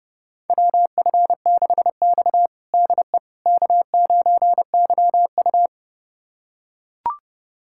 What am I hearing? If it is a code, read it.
Morse code, heard as a keyed tone, about 32 wpm, 700 Hz. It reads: WF6X DE K9YU
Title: USA Callsign DE Callsign - 4 Characters - 30wpm